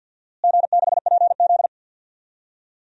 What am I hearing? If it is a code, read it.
Morse code, heard as a keyed tone, about 52 wpm, 700 Hz. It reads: G6PZ
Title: DXCC-100 Call Signs - 4 Characters - Repeated 3x - 50wpm